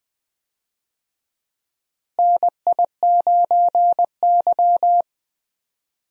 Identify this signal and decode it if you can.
Morse code, heard as a keyed tone, about 20 wpm, 700 Hz. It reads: NI9Y
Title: USA Call Signs - 4 Characters - Repeated 3x - 20wpm